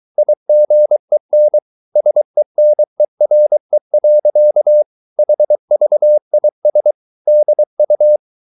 Morse code, heard as a keyed tone, about 23 wpm, 600 Hz. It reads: IGEN SENERE. HVIS DU
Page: Ferie 23 wpm